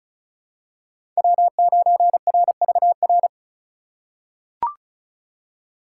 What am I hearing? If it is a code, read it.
Morse code, heard as a keyed tone, about 34 wpm, 700 Hz. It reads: W9RVR
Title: USA Call Signs - 5 Characters - 2x Character Spacing - 35wpm